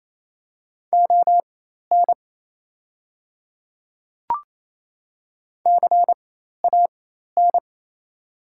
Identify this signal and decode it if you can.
Morse code, heard as a keyed tone, about 28 wpm, spaced as for 13 wpm, 700 Hz. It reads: ON CAN
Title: 4x Character Spacing - Top 100 Words - 28wpm